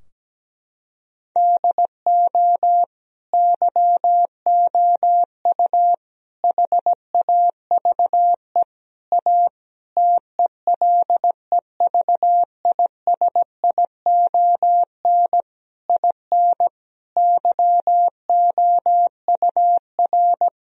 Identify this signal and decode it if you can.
Morse code, heard as a keyed tone, about 17 wpm, 700 Hz. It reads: DO YOU HAVE A TELEVISION IN YOUR
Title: Sentences from Top 1000 Words - Review Part 4 - 17wpm